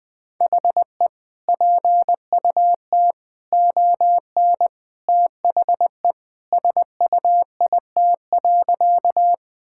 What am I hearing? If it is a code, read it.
Morse code, heard as a keyed tone, about 20 wpm, 700 Hz. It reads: HE PUT ON THE SUIT.
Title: Sentences from Top 1600 Words 20wpm